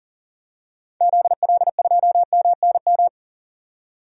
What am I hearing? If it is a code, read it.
Morse code, heard as a keyed tone, about 39 wpm, 700 Hz. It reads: ZL2MNM
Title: DXCC-100 Call Signs - 6 Characters - 40wpm